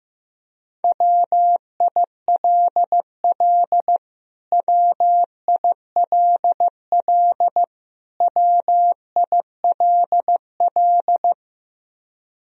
Morse code, heard as a keyed tone, about 15 wpm, 700 Hz. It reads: WILL WILL WILL
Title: Top 100 Words Repeated 15wpm